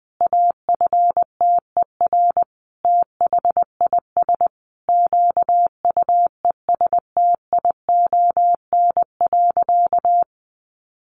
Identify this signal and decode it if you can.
Morse code, heard as a keyed tone, about 20 wpm, 700 Hz. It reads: AFTER THIS QUESTION.
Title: Sentences from Top 1500 Words 20wpm